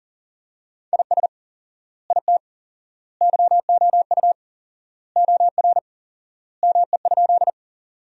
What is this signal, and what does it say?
Morse code, heard as a keyed tone, about 40 wpm, 700 Hz. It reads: IS IT YOU OR ME?
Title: 4x Word-Spacing - Sentences from Top 200 Words - 40wpm